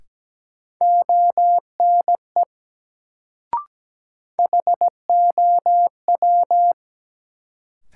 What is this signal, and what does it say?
Morse code, heard as a keyed tone, about 17 wpm, 700 Hz. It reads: ONE HOW
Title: Sets of 2 Words - New Word Drill - 17wpm